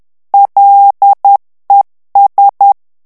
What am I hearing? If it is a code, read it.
Morse code, heard as a keyed tone, about 10 wpm, 800 Hz. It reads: LES
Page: Français - texte en clair 1 - vitesse : 10 mots minute